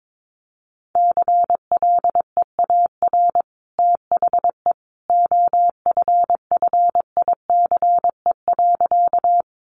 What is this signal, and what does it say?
Morse code, heard as a keyed tone, about 22 wpm, 700 Hz. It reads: CLEAR THE OFFICE.
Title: Sentences from Top 1000 Words 22wpm